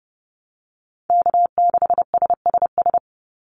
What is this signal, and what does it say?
Morse code, heard as a keyed tone, about 30 wpm, 700 Hz. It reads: K6SSS
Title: Call Signs - 30wpm